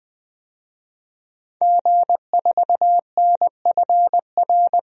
Morse code, heard as a keyed tone, about 20 wpm, 700 Hz. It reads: G4NFR
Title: DXCC-100 Call Signs - 5 Characters - 20wpm